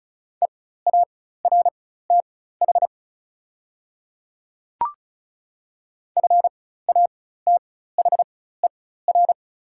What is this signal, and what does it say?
Morse code, heard as a keyed tone, about 34 wpm, spaced as for 16 wpm, 690 Hz. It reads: EARTH FATHER
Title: 4x Character Spacing - Top 200 Words - 35wpm